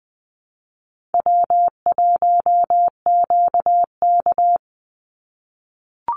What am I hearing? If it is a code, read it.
Morse code, heard as a keyed tone, about 20 wpm, 700 Hz. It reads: W1QK
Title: Easy Call Signs - 20wpm